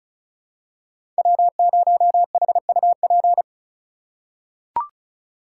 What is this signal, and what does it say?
Morse code, heard as a keyed tone, about 36 wpm, 700 Hz. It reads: W0HUP
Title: USA Call Signs - 4x Character Spacing - 35wpm